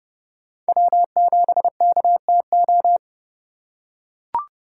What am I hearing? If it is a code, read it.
Morse code, heard as a keyed tone, about 30 wpm, 700 Hz. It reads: W7KTO
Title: USA Call Signs - 5 Characters - 2x Character Spacing - 30wpm